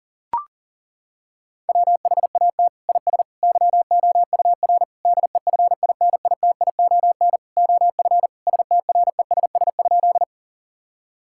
Morse code, heard as a keyed tone, about 40 wpm, 700 Hz. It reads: WHAT IS YOUR DEFINITION OF STRESS?
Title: Sentences from Top 1000 Words - Review Part 6 - 40wpm